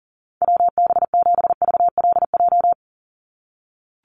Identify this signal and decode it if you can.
Morse code, heard as a keyed tone, about 40 wpm, 700 Hz. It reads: WB7VLJ
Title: USA Call Signs 40wpm